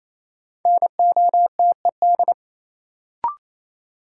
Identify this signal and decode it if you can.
Morse code, heard as a keyed tone, about 28 wpm, 700 Hz. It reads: NOTED
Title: Sets of 5 Words - New Word Drill - 28wpm